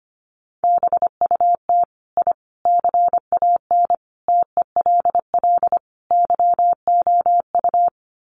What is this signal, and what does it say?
Morse code, heard as a keyed tone, about 25 wpm, 700 Hz. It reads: BUT I CAN TELL YOU
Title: Sets of 5 Words Encore 25wpm